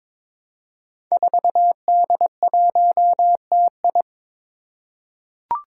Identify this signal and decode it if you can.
Morse code, heard as a keyed tone, about 22 wpm, 700 Hz. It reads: 4D1TI
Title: Five Challenging Letter-Number Combinations 22wpm